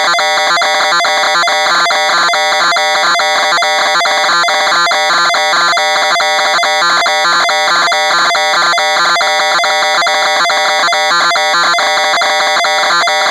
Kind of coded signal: two-tone teleprinter signal